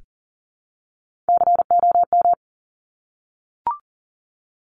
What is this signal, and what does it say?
Morse code, heard as a keyed tone, about 39 wpm, 700 Hz. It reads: COM
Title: Top Three Letter Combinations in Words 40wpm